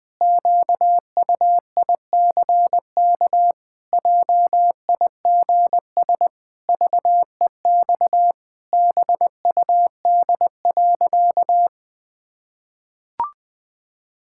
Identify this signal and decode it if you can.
Morse code, heard as a keyed tone, about 20 wpm, 700 Hz. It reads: QUICK JIGS VEX BUD.
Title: Pangram Challenge v2 - 20wpm